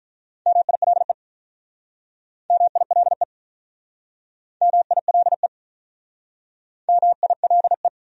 Morse code, heard as a keyed tone, about 46 wpm, 700 Hz. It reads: MILE MILE MILE MILE